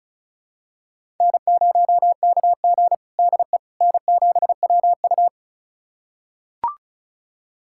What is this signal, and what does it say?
Morse code, heard as a keyed tone, about 36 wpm, 700 Hz. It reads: N0KG DE N7WU
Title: USA Callsign DE Callsign - 4 Characters - 35wpm